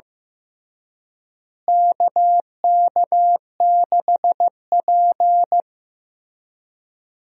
Morse code, heard as a keyed tone, about 15 wpm, 700 Hz. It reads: KK6P
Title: USA Call Signs - 4 Characters - Repeated 2x - 15wpm